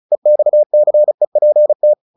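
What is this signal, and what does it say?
Morse code, heard as a keyed tone, about 35 wpm, 600 Hz. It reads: EXCEPT
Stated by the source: Morse practice at 35 wpm from CW med Gnister, sending Ken Follett's novel Never in English.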